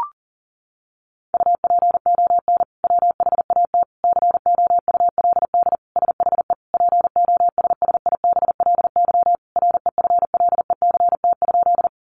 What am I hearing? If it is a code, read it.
Morse code, heard as a keyed tone, about 40 wpm, 700 Hz. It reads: UPON WHAT COULD SHE POSSIBLY REFLECT?
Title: Sentences from Top 1500 Words 40wpm